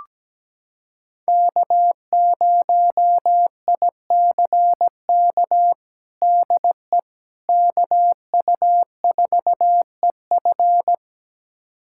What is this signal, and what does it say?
Morse code, heard as a keyed tone, about 17 wpm, 700 Hz. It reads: K0ICK DE KU4EF
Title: USA Callsign DE Callsign - 5 Characters - 17wpm